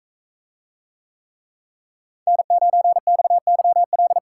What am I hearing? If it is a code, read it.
Morse code, heard as a keyed tone, about 43 wpm, 700 Hz. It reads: N9XYL